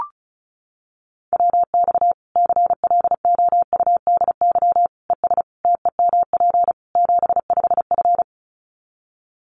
Morse code, heard as a keyed tone, about 34 wpm, 700 Hz. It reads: WX CLOUDY ES TEMP 75F
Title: Common QSO Phrases 35wpm